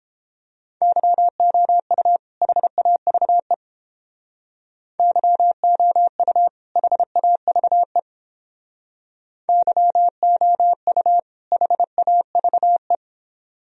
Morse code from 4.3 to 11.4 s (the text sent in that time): YOU HAVE YOU